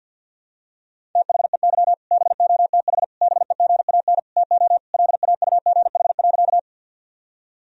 Morse code, heard as a keyed tone, about 46 wpm, 700 Hz. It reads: THEY BOTH BEGAN TO LAUGH.
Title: Sentences from Top 1000 Words - Review Part 1 - 50wpm